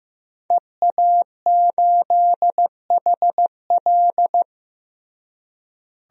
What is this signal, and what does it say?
Morse code, heard as a keyed tone, about 15 wpm, 700 Hz. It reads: EA8HL
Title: DXCC-100 Call Signs - 5 Characters - Repeated 2x - 15wpm